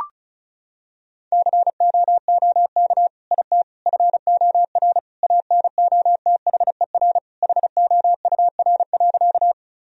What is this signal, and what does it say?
Morse code, heard as a keyed tone, about 34 wpm, 700 Hz. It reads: COOK IT FOR ANOTHER HOUR.
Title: Sentences from Top 1000 Words - Review Part 4 - 35wpm